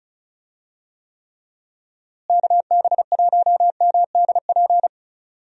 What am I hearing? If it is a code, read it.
Morse code, heard as a keyed tone, about 36 wpm, 700 Hz. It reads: KB1MDP
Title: USA Call Signs - 6 Characters - Repeated 2x - 35wpm